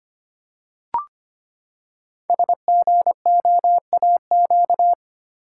Morse code, heard as a keyed tone, about 25 wpm, 700 Hz. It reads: SGOAQ
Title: Code Groups - 5 Letters - 25wpm